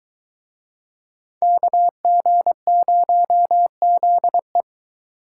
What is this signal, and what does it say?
Morse code, heard as a keyed tone, about 23 wpm, 700 Hz. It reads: KG0ZE